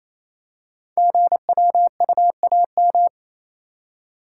Morse code, heard as a keyed tone, about 28 wpm, 700 Hz. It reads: GWUAM
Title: Code Groups - 5 Letters - 28wpm